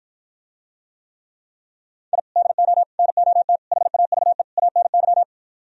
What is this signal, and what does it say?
Morse code, heard as a keyed tone, about 56 wpm, 700 Hz. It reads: I DO NOT HAVE ANY